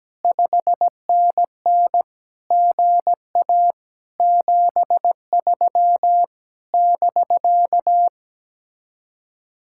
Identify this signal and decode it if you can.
Morse code, heard as a keyed tone, about 17 wpm, 700 Hz. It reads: 5NN GA 73 <BK>
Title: Speed-Racing Common QSO Phrases Encore - 17wpm